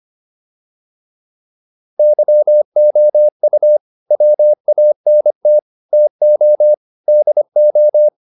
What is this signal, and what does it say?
Morse code, heard as a keyed tone, about 25 wpm, 600 Hz. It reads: YOU WANT TO DO
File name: WD304 25